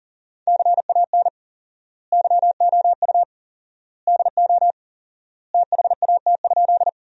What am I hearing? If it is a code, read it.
Morse code, heard as a keyed tone, about 40 wpm, 700 Hz. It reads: CAN YOU DO THAT?
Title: 4x Word-Spacing - Sentences from Top 100 Words - 40wpm